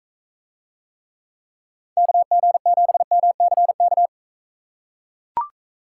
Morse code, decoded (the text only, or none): KG7MCK